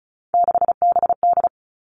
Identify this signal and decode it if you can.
Morse code, heard as a keyed tone, about 36 wpm, 700 Hz. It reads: 6BD
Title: Three Challenging Letter-Number Combinations 35wpm